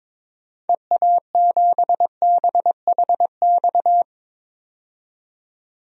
Morse code, heard as a keyed tone, about 23 wpm, 700 Hz. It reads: EA7BHX
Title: DXCC-100 Call Signs - 6 Characters - 22wpm